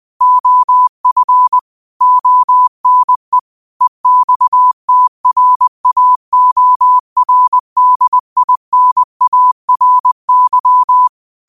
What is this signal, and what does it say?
Morse code, heard as a keyed tone, about 20 wpm, 1000 Hz. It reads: OF ONE EXTRAORDINARY